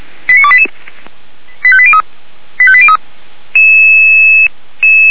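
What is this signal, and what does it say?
Five-tone selective call (ZVEI), heard as quick runs of five tones, tones of about 70 ms. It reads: 89288 85803 85803
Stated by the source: ZVEI five-tone selective call received via SDR, from a fire brigade in Orlamünde.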